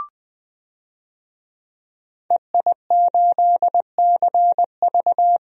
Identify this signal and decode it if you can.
Morse code, heard as a keyed tone, about 21 wpm, 700 Hz. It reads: EI8CV
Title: DXCC-100 Call Signs - 5 Characters - 20wpm